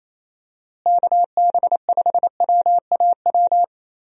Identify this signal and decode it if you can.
Morse code, heard as a keyed tone, about 28 wpm, 700 Hz. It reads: KB5WAW
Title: USA Call Signs - 6 Characters - Repeated 3x - 28wpm